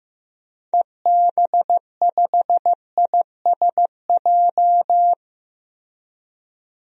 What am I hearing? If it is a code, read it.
Morse code, heard as a keyed tone, about 15 wpm, 700 Hz. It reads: EB5ISJ
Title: DXCC-100 Call Signs - 6 Characters - 15wpm